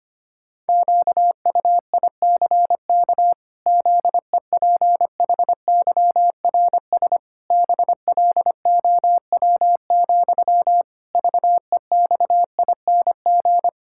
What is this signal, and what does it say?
Morse code, heard as a keyed tone, about 25 wpm, 700 Hz. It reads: QUICK ZEPHYRS BLOW, VEXING
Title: Pangram Challenge v2 - 25wpm